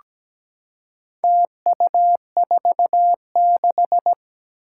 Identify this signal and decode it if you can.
Morse code, heard as a keyed tone, about 17 wpm, 700 Hz. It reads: TU46
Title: Four Challenging Letter-Number Combinations 17wpm